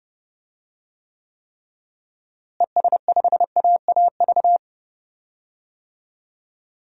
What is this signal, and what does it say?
Morse code, heard as a keyed tone, about 32 wpm, 700 Hz. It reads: ES5AAV
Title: DXCC-100 Call Signs - 6 Characters - Repeated 3x - 30wpm